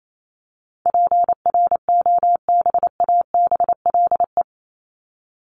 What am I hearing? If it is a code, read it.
Morse code, heard as a keyed tone, about 28 wpm, 700 Hz. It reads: PROBABLE
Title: LY Suffix - Top 100 Words with Base Words 28wpm